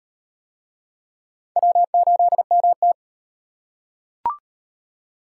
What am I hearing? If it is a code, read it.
Morse code, heard as a keyed tone, about 38 wpm, 700 Hz. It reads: W8MT